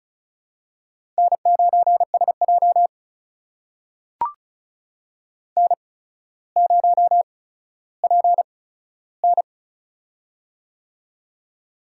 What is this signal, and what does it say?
Morse code, heard as a keyed tone, about 33 wpm, spaced as for 15 wpm, 700 Hz. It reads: N9SJ N0PN
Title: USA Call Signs - 4 Characters - 8x Character Spacing - 35wpm